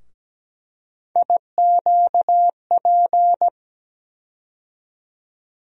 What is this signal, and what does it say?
Morse code, heard as a keyed tone, about 17 wpm, 700 Hz. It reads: IQP
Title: USA Call Sign Postfixes - 3 Letters 17wpm